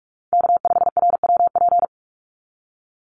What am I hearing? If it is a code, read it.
Morse code, heard as a keyed tone, about 44 wpm, 700 Hz. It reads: K5RWP
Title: NA SOTA Chaser Callsigns 45wpm